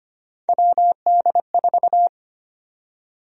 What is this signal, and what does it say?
Morse code, heard as a keyed tone, about 25 wpm, 700 Hz. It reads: WD4
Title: USA Call Sign Prefixes - 25wpm